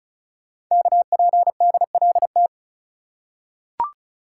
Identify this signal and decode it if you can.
Morse code, heard as a keyed tone, about 36 wpm, 700 Hz. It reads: KPDLT
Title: Code Groups - 5 Letters - 35wpm